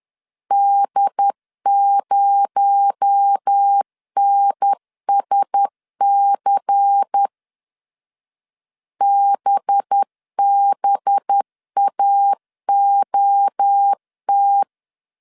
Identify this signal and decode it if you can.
Morse code, heard as a keyed tone, about 10 wpm, 800 Hz. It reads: D0NSC BBAOT
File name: CW / DET / 1 6 1500C S B 10wpm 4137